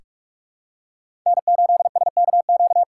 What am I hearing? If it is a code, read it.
Morse code, heard as a keyed tone, about 46 wpm, 700 Hz. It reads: N8SKQ
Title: USA Call Signs - 5 Characters - 3x Character Spacing - 45wpm